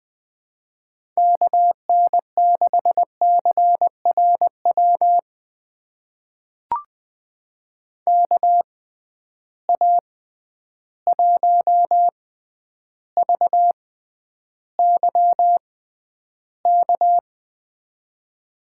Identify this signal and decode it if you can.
Morse code, heard as a keyed tone, about 20 wpm, spaced as for 10 wpm, 700 Hz. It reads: KN6CRW KA1VYK
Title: USA Call Signs - 6 Characters - 6x Character Spacing - 20wpm